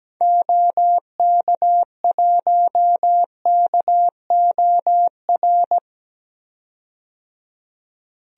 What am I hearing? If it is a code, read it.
Morse code, heard as a keyed tone, about 17 wpm, 700 Hz. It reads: OK1KOR
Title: DXCC-100 Call Signs - 6 Characters - 17wpm